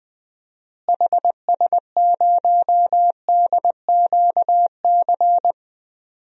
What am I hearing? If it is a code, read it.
Morse code, heard as a keyed tone, about 20 wpm, 700 Hz. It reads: HS0DQC
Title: DXCC-100 Call Signs - 6 Characters - Repeated 2x - 20wpm